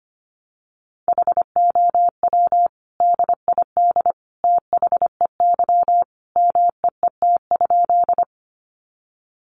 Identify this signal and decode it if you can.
Morse code, heard as a keyed tone, about 25 wpm, 700 Hz. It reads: HOW DID THEY MEET?
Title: Sentences from Top 1100 Words 25wpm